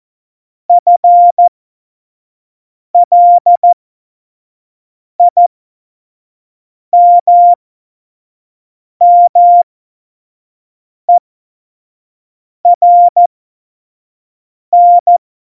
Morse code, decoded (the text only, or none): FLIMMERN